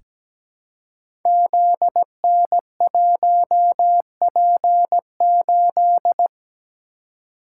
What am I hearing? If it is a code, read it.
Morse code, heard as a keyed tone, about 17 wpm, 690 Hz. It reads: ZN1P8
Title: Code Groups - 5 Letter:Numbers - 17wpm